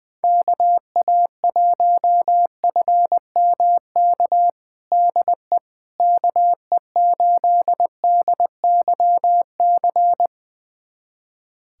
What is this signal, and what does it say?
Morse code, heard as a keyed tone, about 20 wpm, 700 Hz. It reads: KA1FMK DE KE8DYC
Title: USA Callsign DE Callsign - 6 Characters - 20wpm